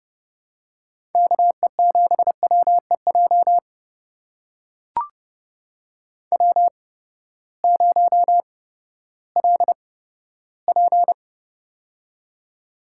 Morse code, heard as a keyed tone, about 30 wpm, spaced as for 15 wpm, 700 Hz. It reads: KE7WEJ W0LP
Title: USA Call Signs - 8x Character Spacing - 30wpm